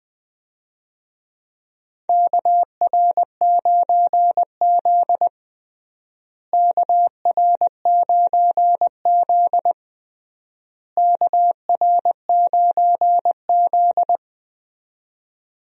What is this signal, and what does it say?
Morse code, heard as a keyed tone, about 20 wpm, 700 Hz. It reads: KR9Z KR9Z KR9Z
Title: USA Call Signs - 4 Characters - Repeated 3x - 20wpm